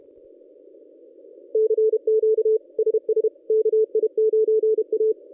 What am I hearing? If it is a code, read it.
Morse code, heard as a keyed tone, about 28 wpm, 440 Hz. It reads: CQ SS KI9A
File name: contests / ss-cw-k5tr-op-n5rz -1000